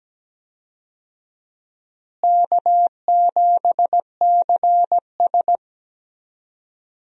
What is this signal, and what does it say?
Morse code, heard as a keyed tone, about 17 wpm, 700 Hz. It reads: K7CS
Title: USA Call Signs - 4 Characters - Repeated 2x - 17wpm